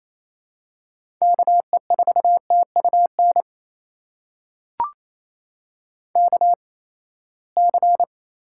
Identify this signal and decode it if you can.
Morse code, heard as a keyed tone, about 27 wpm, spaced as for 19 wpm, 700 Hz. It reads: KE4TUN KC
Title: USA Call Signs - 6 Characters - 8x Character Spacing - 28wpm